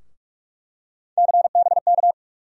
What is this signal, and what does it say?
Morse code, heard as a keyed tone, about 46 wpm, 700 Hz. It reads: CBK
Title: USA Call Sign Postfixes - 3 Letters 45wpm